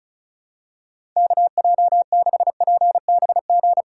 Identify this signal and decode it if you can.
Morse code, heard as a keyed tone, about 36 wpm, 700 Hz. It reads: KJ6PBG
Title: USA Call Signs - 3x Character Spacing - 35wpm